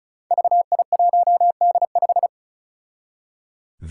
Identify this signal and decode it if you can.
Morse code, heard as a keyed tone, about 36 wpm, 700 Hz. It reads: VI1D5
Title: Five Challenging Letter-Number Combinations 35wpm